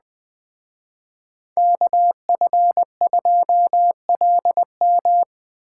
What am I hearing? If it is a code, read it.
Morse code, heard as a keyed tone, about 20 wpm, 700 Hz. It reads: KF2LM